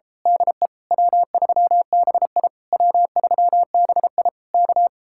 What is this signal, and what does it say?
Morse code, heard as a keyed tone, about 30 wpm, 700 Hz. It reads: DE W3BI W3BI K
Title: Speed-Racing Common QSO Phrases - 28wpm